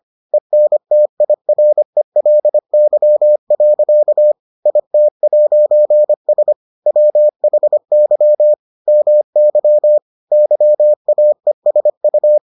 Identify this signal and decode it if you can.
Morse code, heard as a keyed tone, about 25 wpm, 600 Hz. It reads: ENTIRELY. IT'S WHY MY YAESU